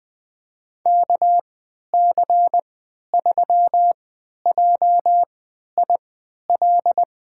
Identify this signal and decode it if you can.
Morse code, heard as a keyed tone, about 21 wpm, spaced as for 12 wpm, 700 Hz. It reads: KC3JIL
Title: USA Call Signs - 6 Characters - 3x Character Spacing - 20wpm